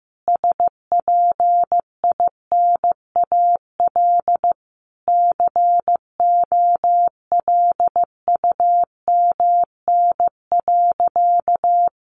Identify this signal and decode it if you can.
Morse code, heard as a keyed tone, about 15 wpm, 700 Hz. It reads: SPINAL COLUMN.
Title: A Sound Of Thunder 15wpm